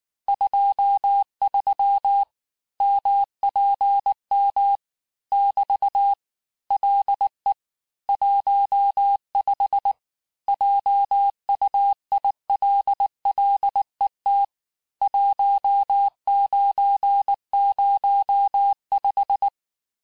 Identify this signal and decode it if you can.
Morse code, heard as a keyed tone, about 19 wpm, 800 Hz. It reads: 23 MPM <BT> LE 15 JUILLET 1905
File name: cours-du-lundi files / cours-1583--texte-clair-18-puis-23-mpm